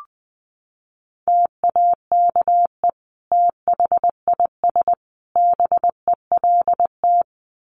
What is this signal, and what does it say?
Morse code, heard as a keyed tone, about 20 wpm, 700 Hz. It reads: TAKE THIS BELT
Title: Sentences from Top 1400 Words 20wpm